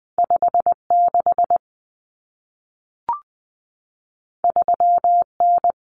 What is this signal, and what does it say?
Morse code, heard as a keyed tone, about 20 wpm, 700 Hz. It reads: 56 3N
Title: Common Cut Numbers - 2-Digit 20wpm